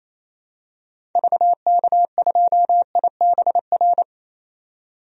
Morse code, heard as a keyed tone, about 28 wpm, 700 Hz. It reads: VK2IBR
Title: DXCC-100 Call Signs - 6 Characters - 28wpm